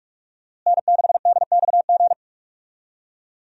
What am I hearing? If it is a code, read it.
Morse code, heard as a keyed tone, about 42 wpm, 700 Hz. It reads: N6DXG
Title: USA Call Signs - 5 Characters - Repeated 3x - 45wpm